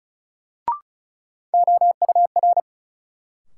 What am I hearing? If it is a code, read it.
Morse code, heard as a keyed tone, about 36 wpm, 700 Hz. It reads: OUR
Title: Sets of 2 Words - New Word Drill - 35wpm